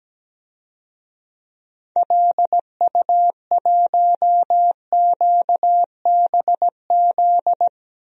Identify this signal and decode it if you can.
Morse code, heard as a keyed tone, about 17 wpm, 700 Hz. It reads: LU1QBZ
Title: DXCC-100 Call Signs - 6 Characters - 17wpm